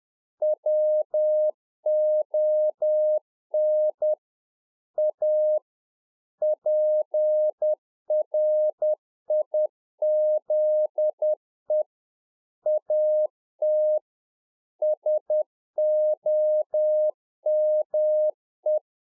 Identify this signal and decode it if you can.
Morse code, heard as a keyed tone, about 10 wpm, 610 Hz. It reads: WON A PRIZE AT SOME